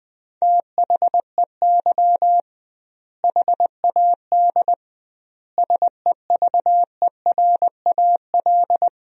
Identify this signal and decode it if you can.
Morse code, heard as a keyed tone, about 20 wpm, 700 Hz. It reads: THEY HAD SEVERAL
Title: 2x Word-Spacing - Sentences from Top 400 Words - 20wpm